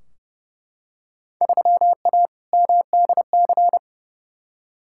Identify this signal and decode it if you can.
Morse code, heard as a keyed tone, about 32 wpm, 700 Hz. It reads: 3A MDC
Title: Field Day Exchanges - 30wpm